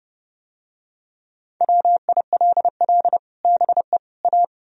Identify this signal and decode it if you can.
Morse code, heard as a keyed tone, about 28 wpm, 700 Hz. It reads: WILL BE A